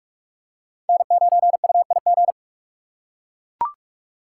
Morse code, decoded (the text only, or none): N9UIG